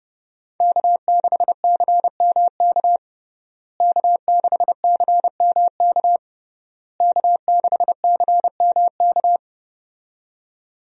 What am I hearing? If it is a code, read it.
Morse code, heard as a keyed tone, about 31 wpm, 700 Hz. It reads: K6CMK K6CMK K6CMK
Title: USA Call Signs - 5 Characters - Repeated 3x - 30wpm